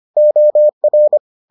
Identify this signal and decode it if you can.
Morse code, heard as a keyed tone, about 25 wpm, 600 Hz. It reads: OR